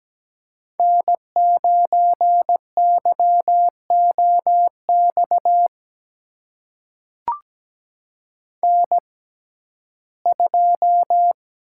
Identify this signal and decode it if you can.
Morse code, heard as a keyed tone, about 17 wpm, spaced as for 12 wpm, 700 Hz. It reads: N9YOX N2
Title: USA Call Signs - 5 Characters - 6x Character Spacing - 17wpm